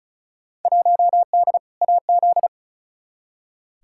Morse code, heard as a keyed tone, about 36 wpm, 700 Hz. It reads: 1D AZ
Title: Field Day Exchanges - 35wpm